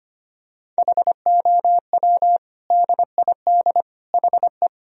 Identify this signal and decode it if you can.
Morse code, heard as a keyed tone, about 25 wpm, 700 Hz. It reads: HOW DID HE
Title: Sentences from Top 1000 Words - Review Part 2 - 25wpm